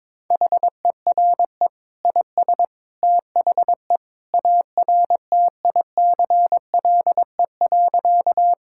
Morse code, heard as a keyed tone, about 21 wpm, 700 Hz. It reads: HERE IS THE ARTICLE.
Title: Sentences from Top 1000 Words - Review Part 5 - 22wpm